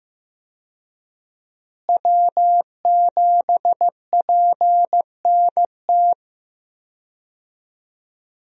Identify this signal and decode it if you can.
Morse code, heard as a keyed tone, about 15 wpm, 700 Hz. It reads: W7PNT
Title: USA Call Signs - 5 Characters - Repeated 2x - 15wpm